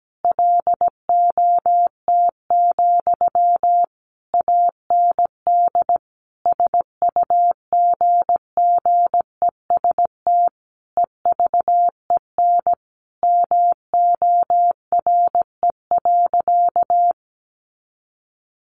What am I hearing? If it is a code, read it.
Morse code, heard as a keyed tone, about 17 wpm, 700 Hz. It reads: LOT, AND SUGGEST EVEN MORE.
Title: Sentences from Top 1300 Words 17wpm